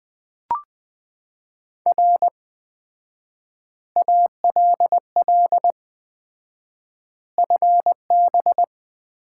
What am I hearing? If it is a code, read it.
Morse code, heard as a keyed tone, about 20 wpm, 700 Hz. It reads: R ALL FB
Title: Common QSO Phrases - 4x Word-Spacing - 20wpm